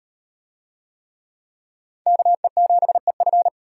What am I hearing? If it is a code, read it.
Morse code, heard as a keyed tone, about 38 wpm, 700 Hz. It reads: KE7EF